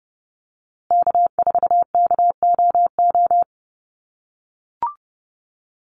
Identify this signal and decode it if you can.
Morse code, heard as a keyed tone, about 30 wpm, 700 Hz. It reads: K4KOO
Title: USA Call Signs Repeated 2x 30wpm